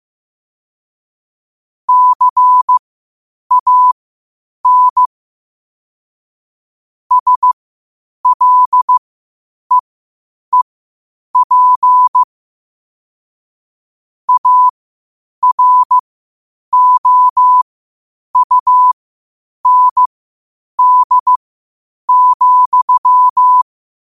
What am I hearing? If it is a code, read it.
Morse code, heard as a keyed tone, about 15 wpm, spaced as for 8 wpm, 1000 Hz. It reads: CAN SLEEP AROUND,
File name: qotdfiles / 8wpm